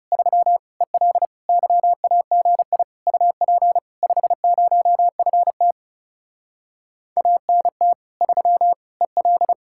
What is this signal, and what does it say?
Morse code, heard as a keyed tone, about 40 wpm, 700 Hz. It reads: 3 EL YAGI UP 50FT ANT 3 EL